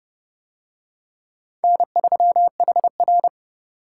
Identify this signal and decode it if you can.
Morse code, heard as a keyed tone, about 29 wpm, 700 Hz. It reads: N3HR